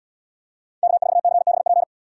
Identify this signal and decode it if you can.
Morse code, heard as a keyed tone, about 74 wpm, 700 Hz. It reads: K4CBP